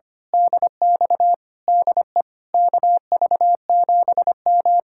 Morse code, heard as a keyed tone, about 24 wpm, 700 Hz. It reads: DX DE KV7M
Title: Speed-Racing Common QSO Phrases - 25wpm